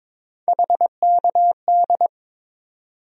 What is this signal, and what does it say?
Morse code, heard as a keyed tone, about 22 wpm, 700 Hz. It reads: HKD
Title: USA Call Sign Postfixes - 3 Letters 22wpm